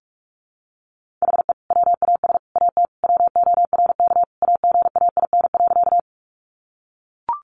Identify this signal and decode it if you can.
Morse code, heard as a keyed tone, about 44 wpm, 700 Hz. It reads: HE WAS AT WORK AGAIN.
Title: Sentences from Top 200 Words 45wpm